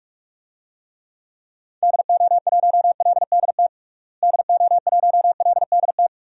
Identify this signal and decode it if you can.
Morse code, heard as a keyed tone, about 47 wpm, 700 Hz. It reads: DO1LDT DO1LDT
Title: DXCC-100 Call Signs - 6 Characters - Repeated 2x - 45wpm